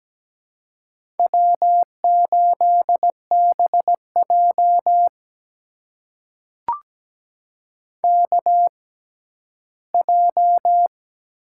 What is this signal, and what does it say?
Morse code, heard as a keyed tone, about 17 wpm, spaced as for 11 wpm, 700 Hz. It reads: W8BJ KJ
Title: USA Call Signs - 6x Character Spacing - 17wpm